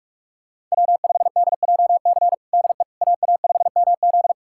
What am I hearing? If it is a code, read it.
Morse code, heard as a keyed tone, about 42 wpm, 700 Hz. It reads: W5DJC DE AA5MZ
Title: USA Callsign DE Callsign - 5 Characters - 45wpm